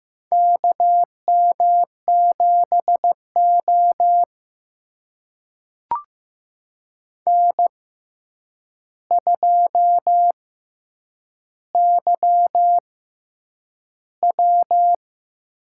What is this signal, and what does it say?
Morse code, heard as a keyed tone, about 15 wpm, spaced as for 8 wpm, 700 Hz. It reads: KM7O N2YW
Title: USA Call Signs - 4 Characters - 6x Character Spacing - 15wpm